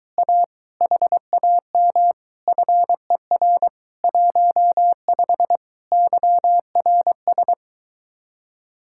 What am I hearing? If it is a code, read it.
Morse code, heard as a keyed tone, about 24 wpm, 700 Hz. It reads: A HAM FER 15 YRS